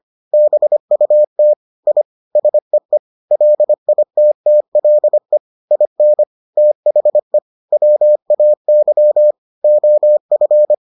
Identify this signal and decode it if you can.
Morse code, heard as a keyed tone, about 25 wpm, 600 Hz. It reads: BUT I SEE LITTLE IN THE WAY OF